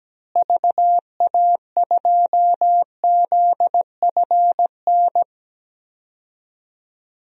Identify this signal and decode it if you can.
Morse code, heard as a keyed tone, about 17 wpm, 700 Hz. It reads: VA2ZFN
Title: DXCC-100 Call Signs - 6 Characters - Repeated 3x - 17wpm